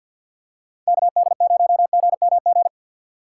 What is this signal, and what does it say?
Morse code, heard as a keyed tone, about 52 wpm, 700 Hz. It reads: KD0GMG
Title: USA Call Signs - 6x Character Spacing - 50wpm